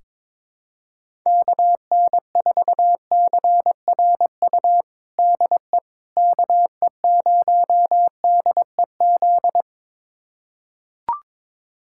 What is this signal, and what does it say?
Morse code, heard as a keyed tone, about 22 wpm, 700 Hz. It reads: KN4CRU DE KE0DEZ
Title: USA Callsign DE Callsign - 6 Characters - 22wpm